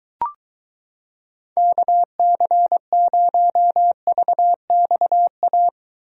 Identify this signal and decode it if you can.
Morse code, heard as a keyed tone, about 23 wpm, 700 Hz. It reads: KC0VXA